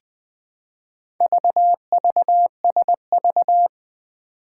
Morse code, heard as a keyed tone, about 20 wpm, 700 Hz. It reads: VVSV
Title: Four Challenging Letter-Number Combinations 20wpm